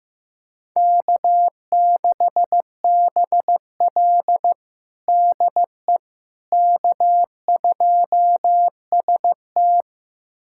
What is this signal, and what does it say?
Morse code, heard as a keyed tone, about 15 wpm, 700 Hz. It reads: K6BL DE K2ST
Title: USA Callsign DE Callsign - 4 Characters - 15wpm